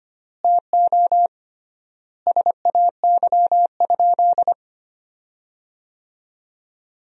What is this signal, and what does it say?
Morse code, heard as a keyed tone, about 25 wpm, 700 Hz. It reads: TO SAY?
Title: 3x Word-Spacing - Sentences from Top 200 Words - 25wpm